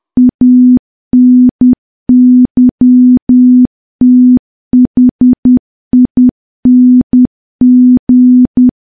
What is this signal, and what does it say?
Morse code, heard as a keyed tone, about 10 wpm, 250 Hz. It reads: ANYTHING
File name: morse / morse-training-10WPM